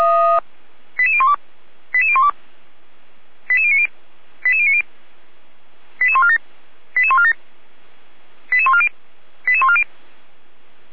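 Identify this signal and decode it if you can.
Five-tone selective call (ZVEI), heard as quick runs of five tones, tones of about 70 ms. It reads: 88012 88012 88090 88090 88147 88147 88140 88140
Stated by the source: ZVEI five-tone selective call 88149 received by SDR.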